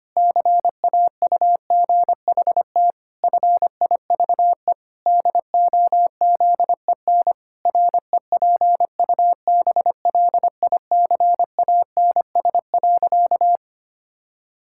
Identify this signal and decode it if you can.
Morse code, heard as a keyed tone, about 25 wpm, 700 Hz. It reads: CAUGHT FIVE DOZEN REPUBLICANS.
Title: Pangram Challenge v2 - 25wpm